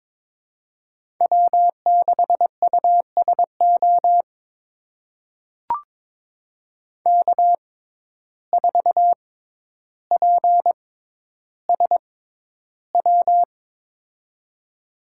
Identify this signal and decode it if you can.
Morse code, heard as a keyed tone, about 22 wpm, spaced as for 11 wpm, 700 Hz. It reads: W6USO K4PSW
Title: USA Call Signs - 5 Characters - 6x Character Spacing - 22wpm